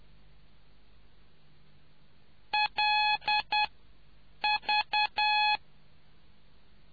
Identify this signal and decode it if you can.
Morse code, heard as a keyed tone, about 10 wpm, spaced as for 7 wpm, 840 Hz. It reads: LV